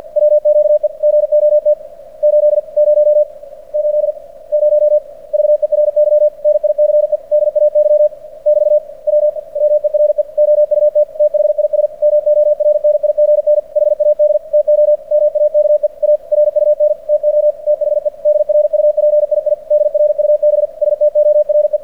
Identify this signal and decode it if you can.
Morse code, heard as a keyed tone, about 46 wpm, 600 Hz. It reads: WQE GOT 80 80 VERY NICE DAY <BT> WMERE OUT THIS MORNIGN HAD TO MAKE A RUN TO THE DRUPII DRUB STORE
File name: archived / CRSnet